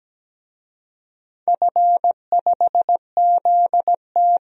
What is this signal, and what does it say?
Morse code, heard as a keyed tone, about 17 wpm, 700 Hz. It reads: F5ZT
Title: DXCC-100 Call Signs - 4 Characters - 17wpm